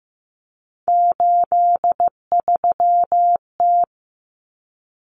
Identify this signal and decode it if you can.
Morse code, heard as a keyed tone, about 15 wpm, 700 Hz. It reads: 83T
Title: Common Cut Numbers - 3-Digit 15wpm